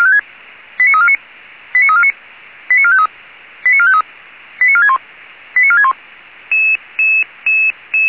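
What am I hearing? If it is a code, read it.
Five-tone selective call (ZVEI), heard as quick runs of five tones, tones of about 70 ms. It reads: 89349 89349 89453 89453 89461 89461
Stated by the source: ZVEI five-tone selective call received by SDR.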